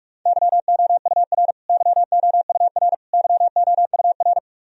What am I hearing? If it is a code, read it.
Morse code, heard as a keyed tone, about 43 wpm, 700 Hz. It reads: YOUR YOUR YOUR
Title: Top 100 Words Repeated 45wpm